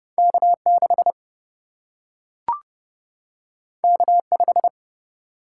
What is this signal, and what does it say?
Morse code, heard as a keyed tone, about 30 wpm, 700 Hz. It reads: K 6 K 5
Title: Single-Letter Single-Number 30wpm